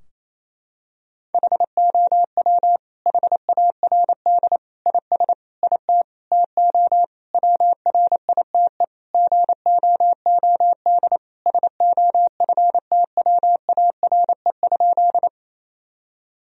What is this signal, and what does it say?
Morse code, heard as a keyed tone, about 28 wpm, 700 Hz. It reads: HOW HARD IS IT TO WRITE GOOD SOFTWARE?
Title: Sentences from Top 1000 Words - Review Part 3 - 28wpm